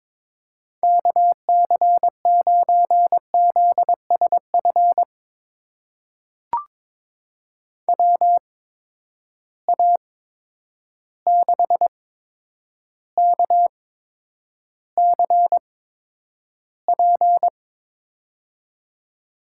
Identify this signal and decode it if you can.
Morse code, heard as a keyed tone, about 22 wpm, spaced as for 9 wpm, 700 Hz. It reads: KC9ZSF WA6KCP
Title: USA Call Signs - 6 Characters - 8x Character Spacing - 22wpm